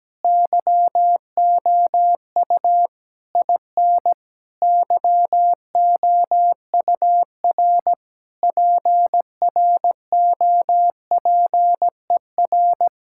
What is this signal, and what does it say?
Morse code, heard as a keyed tone, about 17 wpm, 700 Hz. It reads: YOU IN YOUR PROPER
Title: Sentences from Top 1000 Words - Review Part 5 - 17wpm